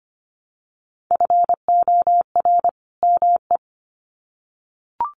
Morse code, heard as a keyed tone, about 25 wpm, 700 Hz. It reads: FOR ME
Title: Sets of 2 Words 25wpm